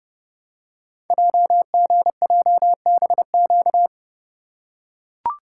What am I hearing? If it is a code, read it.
Morse code, heard as a keyed tone, about 30 wpm, 700 Hz. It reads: JGJBQ
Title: Code Groups - 5 Letters - 30wpm